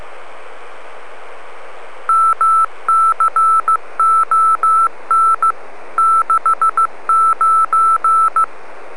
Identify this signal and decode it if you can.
Morse code, heard as a keyed tone, about 15 wpm, 1320 Hz. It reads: MCON 69